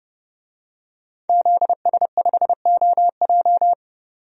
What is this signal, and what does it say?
Morse code, heard as a keyed tone, about 29 wpm, 700 Hz. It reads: ZS5OJ
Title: DXCC-100 Call Signs - 5 Characters - 30wpm